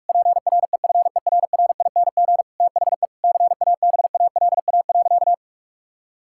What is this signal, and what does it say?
Morse code, heard as a keyed tone, about 41 wpm, 700 Hz. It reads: PREFERRING THE CABALA.
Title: Pangram Challenge v2 - 45wpm